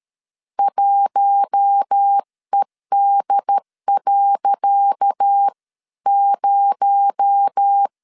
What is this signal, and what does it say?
Morse code, heard as a keyed tone, about 13 wpm, 800 Hz. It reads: 1ED.0
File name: LEZIONI / 10 SIGNS / 10C DOT 12wpm 2451